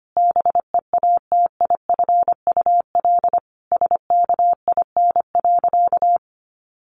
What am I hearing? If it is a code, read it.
Morse code, heard as a keyed tone, about 25 wpm, 700 Hz. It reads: BEATIFUL SKIN.
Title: Sentences from Top 1300 Words 25wpm